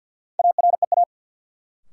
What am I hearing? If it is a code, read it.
Morse code, heard as a keyed tone, about 51 wpm, 700 Hz. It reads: AREA
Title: Sentences from Top 1000 Words - Review Part 2 - New Word Drill - 50wpm